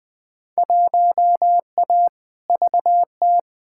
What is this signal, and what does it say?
Morse code, heard as a keyed tone, about 20 wpm, 700 Hz. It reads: 1A VT
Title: Field Day Exchanges Encore - 20wpm